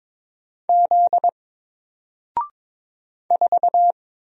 Morse code, heard as a keyed tone, about 22 wpm, 700 Hz. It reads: Z 4
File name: Single-Letter-Number-22wpm